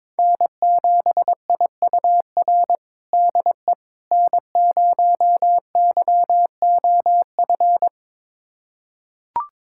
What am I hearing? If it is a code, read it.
Morse code, heard as a keyed tone, about 22 wpm, 700 Hz. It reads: N7IUR DE N0YOF
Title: USA Callsign DE Callsign - 5 Characters - 22wpm